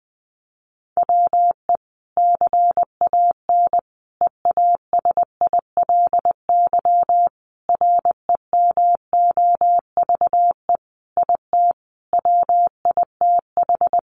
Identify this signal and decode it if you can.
Morse code, heard as a keyed tone, about 20 wpm, 700 Hz. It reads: WE CAN EASILY REMOVE IT WITH
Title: Sentences from Top 1200 Words 20wpm